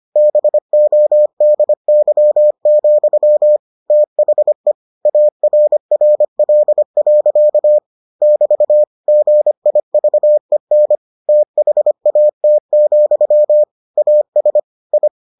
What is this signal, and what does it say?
Morse code, heard as a keyed tone, about 25 wpm, 600 Hz. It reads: BODY, THE ARRL. <BT> GIVEN THAT, AS I